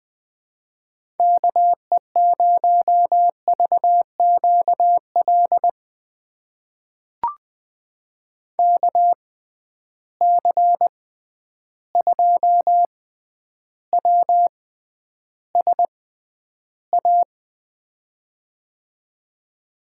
Morse code, heard as a keyed tone, about 20 wpm, spaced as for 10 wpm, 700 Hz. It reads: KE0VQL KC2WSA
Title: USA Call Signs - 6 Characters - 6x Character Spacing - 20wpm